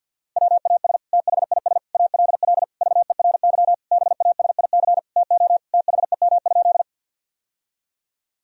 Morse code, heard as a keyed tone, about 50 wpm, 700 Hz. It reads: WAS THIS ALL VERY BASIC TO THEM?
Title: Sentences from Top 1000 Words - Review Part 3 - 50wpm